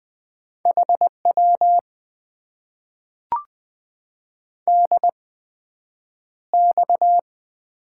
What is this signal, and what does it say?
Morse code, heard as a keyed tone, about 20 wpm, spaced as for 9 wpm, 700 Hz. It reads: HW DX
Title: Top 90 QSO Elements - 8x Character Spacing - 20wpm